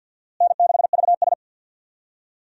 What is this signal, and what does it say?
Morse code, heard as a keyed tone, about 46 wpm, 700 Hz. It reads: N6VS
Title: USA Call Signs - 4 Characters - Repeated 3x - 50wpm